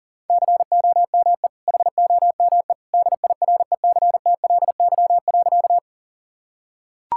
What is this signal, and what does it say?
Morse code, heard as a keyed tone, about 40 wpm, 700 Hz. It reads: COME HOME DIRECTLY.
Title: Sentences from Top 1000 Words - Review Part 5 - 40wpm